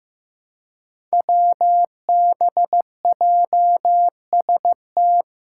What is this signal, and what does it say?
Morse code, heard as a keyed tone, about 15 wpm, 700 Hz. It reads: WBJST
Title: Five Challenging Letter-Number Combinations 15wpm